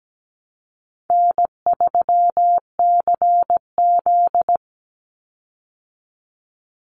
Morse code, heard as a keyed tone, about 17 wpm, 700 Hz. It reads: N3CZ
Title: Easy Call Signs Repeated 3x - 17wpm